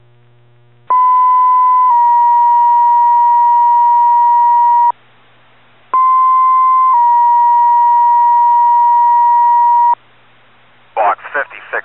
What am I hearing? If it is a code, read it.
Two-tone page, heard as a one-second tone, then a three-second tone, about 1010 then 950 Hz, about 1040 then 950 Hz, with speech. It is heard from 0.9 to 9.9 s.